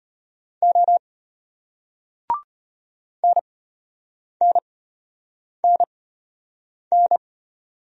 Morse code, heard as a keyed tone, about 33 wpm, spaced as for 8 wpm, 700 Hz. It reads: O NNNN